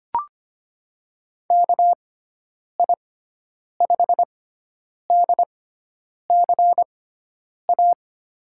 Morse code, heard as a keyed tone, about 25 wpm, spaced as for 9 wpm, 700 Hz. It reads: KI5DCA
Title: USA Call Signs - 6x Character Spacing - 25wpm